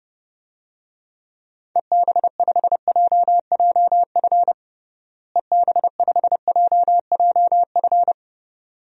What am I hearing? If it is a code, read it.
Morse code, heard as a keyed tone, about 30 wpm, 700 Hz. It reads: EB5JJF EB5JJF
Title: DXCC-100 Call Signs - 6 Characters - Repeated 2x - 30wpm